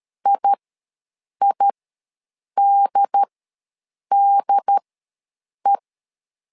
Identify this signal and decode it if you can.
Morse code, heard as a keyed tone, about 13 wpm, spaced as for 7 wpm, 800 Hz. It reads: IIDDE
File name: LEZIONI / 2 UHDN / 2 UHD ws 12wpm 1722